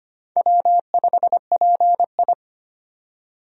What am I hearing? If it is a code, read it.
Morse code, heard as a keyed tone, about 25 wpm, 700 Hz. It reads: W5PI
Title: USA Call Signs - 4 Characters - 25wpm